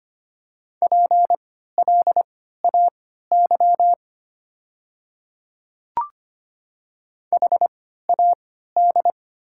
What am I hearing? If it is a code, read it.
Morse code, heard as a keyed tone, about 25 wpm, spaced as for 14 wpm, 700 Hz. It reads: PLAY HAD
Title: 3x Character Spacing - Top 200 Words - 25wpm